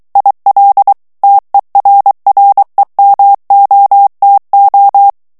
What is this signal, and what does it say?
Morse code, heard as a keyed tone, about 23 wpm, 800 Hz. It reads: IL TERREMOTO
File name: ITALIEN texte clair 1 vitesse 22 mots minute